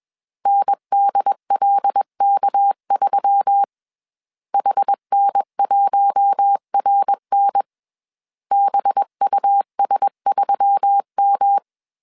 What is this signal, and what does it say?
Morse code, heard as a keyed tone, about 21 wpm, 800 Hz. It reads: NDLK3 HN1RN BUS3M
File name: CW / DET / 1 6 1500C S B 20wpm 2048